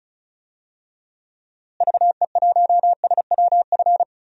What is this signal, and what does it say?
Morse code, heard as a keyed tone, about 36 wpm, 700 Hz. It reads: VE1SWF
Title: DXCC-100 Call Signs - 6 Characters - Repeated 2x - 35wpm